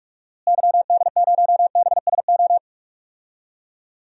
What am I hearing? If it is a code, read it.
Morse code, heard as a keyed tone, about 41 wpm, 700 Hz. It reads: YD0BSO
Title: DXCC-100 Call Signs - 6 Characters - Repeated 3x - 45wpm